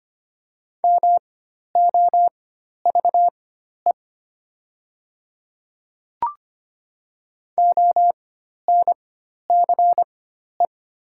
Morse code, heard as a keyed tone, about 25 wpm, spaced as for 12 wpm, 700 Hz. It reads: MOVE ONCE
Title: 4x Character Spacing - Top 500 Words - 25wpm